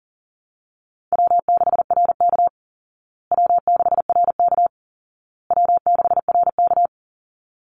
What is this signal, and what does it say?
Morse code, heard as a keyed tone, about 40 wpm, 700 Hz. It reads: W6RK W6RK W6RK
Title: Easy Call Signs Repeated 3x - 40wpm